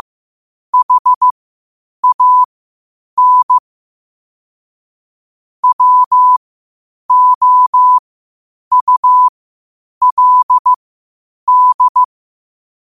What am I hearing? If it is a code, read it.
Morse code, heard as a keyed tone, about 15 wpm, spaced as for 8 wpm, 1000 Hz. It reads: HAN WOULD
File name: qotdfiles / 8wpm